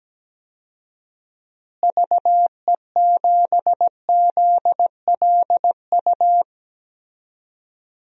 Morse code, heard as a keyed tone, about 17 wpm, 700 Hz. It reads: VE7ZLU
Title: DXCC-100 Call Signs - 6 Characters - Repeated 3x - 17wpm